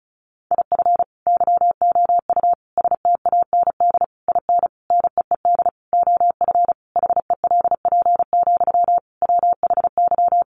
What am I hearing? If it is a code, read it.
Morse code, heard as a keyed tone, about 34 wpm, 700 Hz. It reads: IF YOU STAND IN NEED OF HELP, WHY
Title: Sentences from Top 1200 Words 35wpm